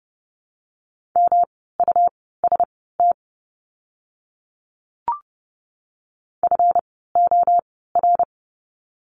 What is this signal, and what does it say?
Morse code, heard as a keyed tone, about 30 wpm, spaced as for 17 wpm, 700 Hz. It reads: MUST FOR
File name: course / 30:10wpm / 237 Lesson 38f - Letter q - Words 30wpm